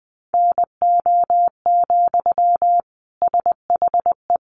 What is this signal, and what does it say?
Morse code, heard as a keyed tone, about 20 wpm, 700 Hz. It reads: NO, SHE
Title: Sentences from Top 800 Words 20wpm